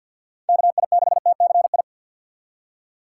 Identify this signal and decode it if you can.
Morse code, heard as a keyed tone, about 52 wpm, 700 Hz. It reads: KI6TCI
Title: USA Call Signs - 6 Characters - 4x Character Spacing - 50wpm